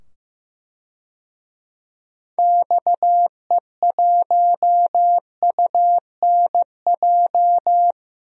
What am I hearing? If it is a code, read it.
Morse code, heard as a keyed tone, about 15 wpm, 700 Hz. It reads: XE1UNJ
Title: DXCC-100 Call Signs - 6 Characters - 15wpm